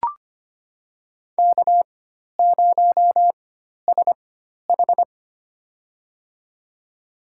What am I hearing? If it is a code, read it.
Morse code, heard as a keyed tone, about 25 wpm, spaced as for 12 wpm, 700 Hz. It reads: K0SH
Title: USA Call Signs - 4 Characters - 4x Character Spacing - 25wpm